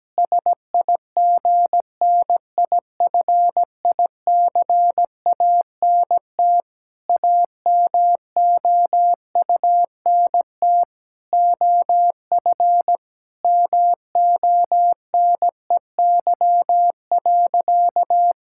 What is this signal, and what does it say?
Morse code, heard as a keyed tone, about 17 wpm, 700 Hz. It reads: SIGNIFICANT AMOUNT OF MONEY.
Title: Sentences from Top 1000 Words - Review Part 6 - 17wpm